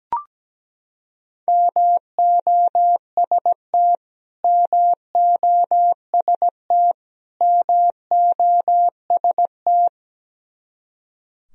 Morse code, heard as a keyed tone, about 17 wpm, 700 Hz. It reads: MOST MOST MOST
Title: Top 100 Words Repeated 17wpm